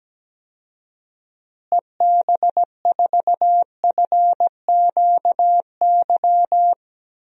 Morse code, heard as a keyed tone, about 17 wpm, 700 Hz. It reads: EB4FQY
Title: DXCC-100 Call Signs - 6 Characters - Repeated 3x - 17wpm